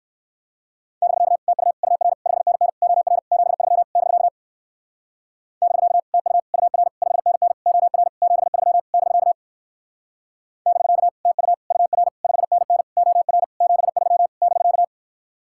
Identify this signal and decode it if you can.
Morse code, heard as a keyed tone, about 60 wpm, 700 Hz. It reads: <BK> TU UR 5NN OR 73 <BK> <BK> TU UR 5NN OR 73 <BK> <BK> TU UR 5NN OR 73 <BK>